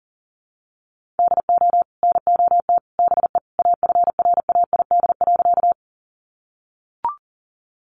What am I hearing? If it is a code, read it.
Morse code, heard as a keyed tone, about 40 wpm, 700 Hz. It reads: DO NOT BE AFRAID.
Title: Sentences from Top 1500 Words 40wpm